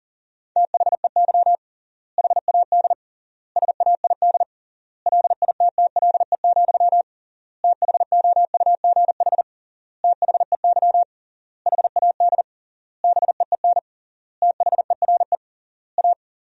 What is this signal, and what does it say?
Morse code, heard as a keyed tone, about 40 wpm, 700 Hz. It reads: THEY HAD SAID LITTLE, THOUGH THEY HAD BEEN THERE A
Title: 3x Word-Spacing - Sentences from Top 400 Words - 40wpm